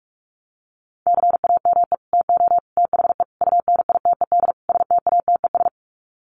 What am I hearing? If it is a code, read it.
Morse code, heard as a keyed tone, about 44 wpm, 700 Hz. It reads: CAME TO THE UNITED STATES
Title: Sets of 5 Words Encore2 45wpm